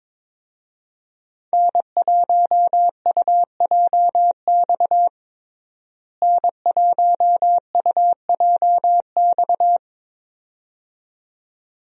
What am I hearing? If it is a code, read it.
Morse code, heard as a keyed tone, about 22 wpm, 700 Hz. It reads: N1UJX N1UJX
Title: USA Call Signs - 5 Characters - Repeated 2x - 22wpm